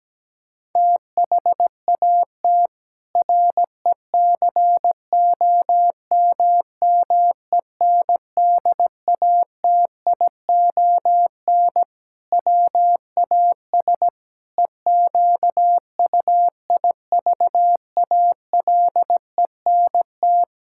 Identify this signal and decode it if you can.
Morse code, heard as a keyed tone, about 17 wpm, 700 Hz. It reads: THAT RECOMMENDATION WAS EQUIVALENT
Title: Sentences from Top 1600 Words 17wpm